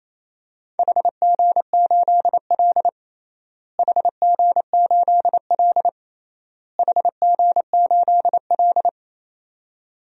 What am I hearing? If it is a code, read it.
Morse code, heard as a keyed tone, about 28 wpm, 700 Hz. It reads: HG8L HG8L HG8L
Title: DXCC-100 Call Signs - 4 Characters - Repeated 3x - 28wpm